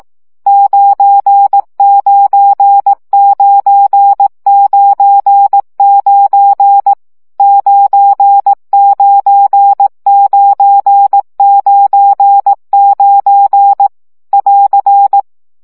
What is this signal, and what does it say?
Morse code, heard as a keyed tone, about 18 wpm, 800 Hz. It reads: 99999 99999 <AR>
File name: CW gyakorlas 9 15 baud